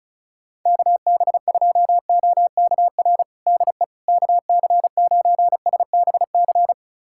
Morse code, heard as a keyed tone, about 36 wpm, 700 Hz. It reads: KB2OKR DE KC9SBC
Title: USA Callsign DE Callsign - 6 Characters - 35wpm